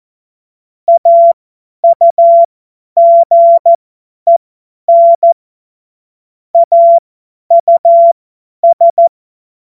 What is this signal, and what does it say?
Morse code, heard as a keyed tone, about 14 wpm, spaced as for 10 wpm, 680 Hz. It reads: AUGEN AUS
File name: Morsekurs / Witze / Witz 19 März 2024 50 F70